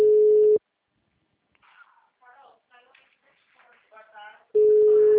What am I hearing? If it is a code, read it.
Ringback tone (Europe, 425 Hz), with speech.
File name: out- -2005- - - .99712